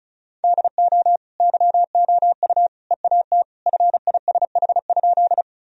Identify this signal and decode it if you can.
Morse code, heard as a keyed tone, about 36 wpm, 700 Hz. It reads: DO YOU EAT FISH?
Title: Sentences from Top 1000 Words - Review Part 1 - 35wpm